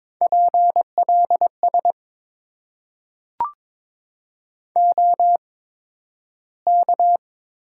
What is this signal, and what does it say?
Morse code, heard as a keyed tone, about 23 wpm, spaced as for 12 wpm, 700 Hz. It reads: PLS OK
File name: Top 90 QSO Elements - 8x Character Spacing - 22wpm